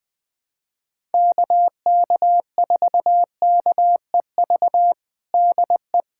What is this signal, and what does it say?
Morse code, heard as a keyed tone, about 20 wpm, 700 Hz. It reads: KK4KEV DE
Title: USA Callsign DE Callsign - 6 Characters - 20wpm